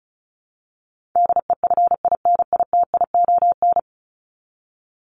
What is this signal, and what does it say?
Morse code, heard as a keyed tone, about 34 wpm, 700 Hz. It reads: DEFINITION
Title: Top 501-750 Words 35wpm